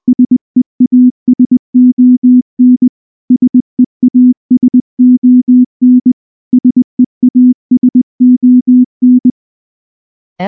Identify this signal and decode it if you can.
Morse code, heard as a keyed tone, about 19 wpm, 250 Hz. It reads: SEASON SEASON SEASON
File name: morse / morse-training-20WPM